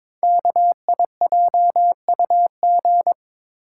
Morse code, heard as a keyed tone, about 22 wpm, 700 Hz. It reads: KIJUG
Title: Code Groups - 5 Letters - 22wpm